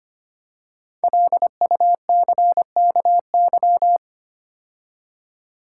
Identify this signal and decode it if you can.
Morse code, heard as a keyed tone, about 25 wpm, 700 Hz. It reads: LUCKY
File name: 5 Letter Words - First 3 Letters Given - 25wpm